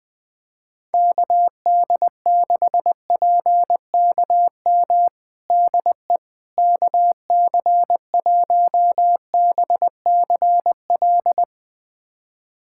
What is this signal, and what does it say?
Morse code, heard as a keyed tone, about 20 wpm, 700 Hz. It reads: KD6PKM DE KC1BCL
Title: USA Callsign DE Callsign 20wpm